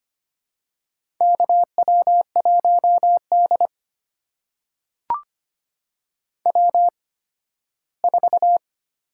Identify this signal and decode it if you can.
Morse code, heard as a keyed tone, about 25 wpm, spaced as for 15 wpm, 700 Hz. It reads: KW1D W4
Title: USA Call Signs - 4 Characters - 8x Character Spacing - 25wpm